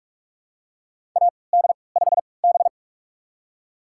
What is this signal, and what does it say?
Morse code, heard as a keyed tone, about 46 wpm, spaced as for 26 wpm, 700 Hz. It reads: AD5B
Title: USA Call Signs - 4 Characters - 3x Character Spacing - 45wpm